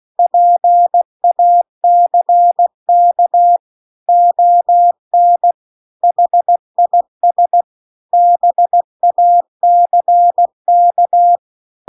Morse code, heard as a keyed tone, about 16 wpm, 700 Hz. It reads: PACK ON HIS BACK